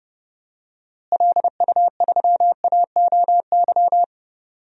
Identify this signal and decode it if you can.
Morse code, heard as a keyed tone, about 30 wpm, 700 Hz. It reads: LU3AOY
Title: DXCC-100 Call Signs - 6 Characters - Repeated 3x - 30wpm